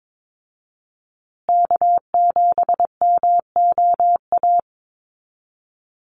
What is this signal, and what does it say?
Morse code, heard as a keyed tone, about 22 wpm, 700 Hz. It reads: K7MOA
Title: Call Signs Repeated 2x - 22wpm